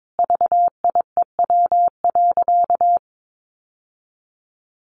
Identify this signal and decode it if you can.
Morse code, heard as a keyed tone, about 22 wpm, 700 Hz. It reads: VIEW.
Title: Sentences from Top 1100 Words 22wpm